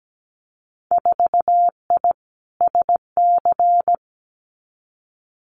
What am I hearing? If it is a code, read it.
Morse code, heard as a keyed tone, about 17 wpm, 700 Hz. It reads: 4I SC
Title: Winter Field Day Exchanges 17wpm